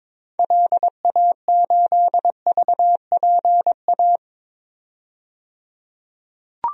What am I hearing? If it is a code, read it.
Morse code, heard as a keyed tone, about 22 wpm, 700 Hz. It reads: LA8VPA
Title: DXCC-100 Call Signs - 6 Characters - 22wpm